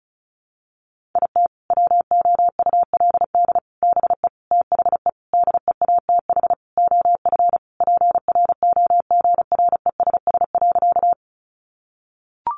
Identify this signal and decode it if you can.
Morse code, heard as a keyed tone, about 34 wpm, 700 Hz. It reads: IT WOULD BE THE DEATH OF PROGRESS.
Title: Sentences from Top 1400 Words 35wpm